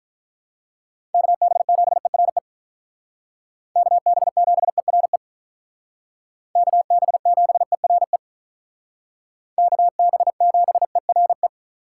Morse code, heard as a keyed tone, about 43 wpm, 700 Hz. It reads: KB7ERE KB7ERE KB7ERE KB7ERE
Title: Speed Racing - USA 2x3 Call Signs 35wpm